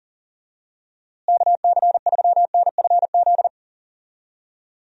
Morse code, heard as a keyed tone, about 40 wpm, 700 Hz. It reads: KC3NFZ
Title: USA Call Signs - 6 Characters - Repeated 3x - 40wpm